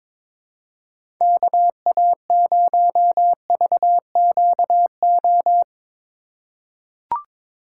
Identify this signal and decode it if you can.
Morse code, heard as a keyed tone, about 21 wpm, 700 Hz. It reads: KA0VQO
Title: USA Call Signs - 6 Characters - 2x Character Spacing - 22wpm